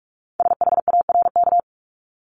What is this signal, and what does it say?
Morse code, heard as a keyed tone, about 46 wpm, 700 Hz. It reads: SHARK
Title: Multi-letter Phonemes Repeated 3x 45wpm